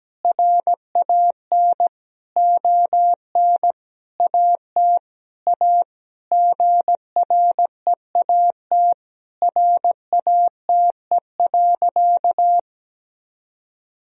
Morse code, heard as a keyed tone, about 17 wpm, 700 Hz. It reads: RAN ON AT A GREAT RATE.
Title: Sentences from Top 1000 Words - Review Part 4 - 17wpm